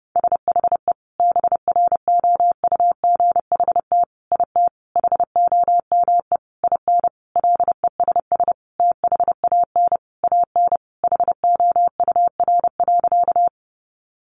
Morse code, heard as a keyed tone, about 30 wpm, 700 Hz. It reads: SHE BROUGHT IT HOME IN LESS THAN AN HOUR.
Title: Sentences from Top 500 Words 30wpm